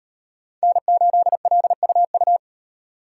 Morse code, heard as a keyed tone, about 38 wpm, 700 Hz. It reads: N8LUU